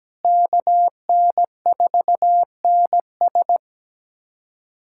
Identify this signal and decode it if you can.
Morse code, heard as a keyed tone, about 17 wpm, 700 Hz. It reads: KN4NS
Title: USA Callsign DE Callsign - 5 Characters - 17wpm